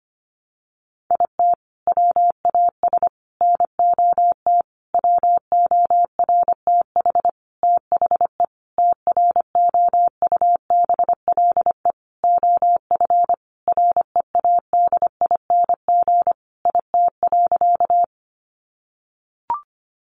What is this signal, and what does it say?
Morse code, heard as a keyed tone, about 25 wpm, 700 Hz. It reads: IT WAS NOT WORTH THE TROUBLE OF READING IT.
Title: Sentences from Top 1200 Words 25wpm